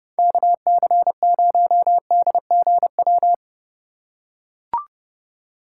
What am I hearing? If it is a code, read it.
Morse code, heard as a keyed tone, about 30 wpm, 700 Hz. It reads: KC0DGW